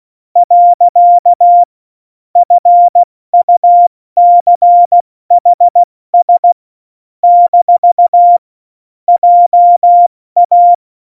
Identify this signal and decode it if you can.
Morse code, heard as a keyed tone, about 16 wpm, 700 Hz. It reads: . FUCHS - JA
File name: Morsekurs / Witze / Witz 28. August 2018, 70 F80